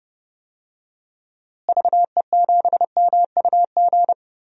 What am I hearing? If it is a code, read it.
Morse code, heard as a keyed tone, about 32 wpm, 700 Hz. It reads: VE7MUG
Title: DXCC-100 Call Signs - 6 Characters - Repeated 3x - 30wpm